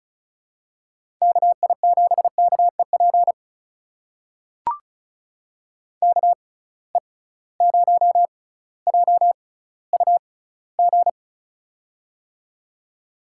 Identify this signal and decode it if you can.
Morse code, heard as a keyed tone, about 36 wpm, spaced as for 18 wpm, 700 Hz. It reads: KI7KEP KE0JUG
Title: USA Call Signs - 6 Characters - 6x Character Spacing - 35wpm